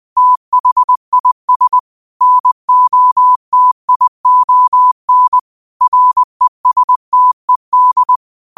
Morse code, heard as a keyed tone, about 20 wpm, 1000 Hz. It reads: THIS NOTION RESTED
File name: qotdfiles / 20wpm